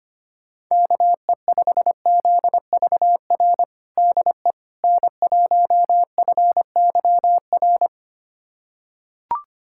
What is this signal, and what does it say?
Morse code, heard as a keyed tone, about 25 wpm, 700 Hz. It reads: KE5ZVR DE N1FYR
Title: USA Callsign DE Callsign 25wpm